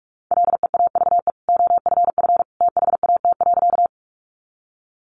Morse code, heard as a keyed tone, about 44 wpm, 700 Hz. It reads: LEAVE OFF THAT.
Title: Sentences from Top 400 Words 45wpm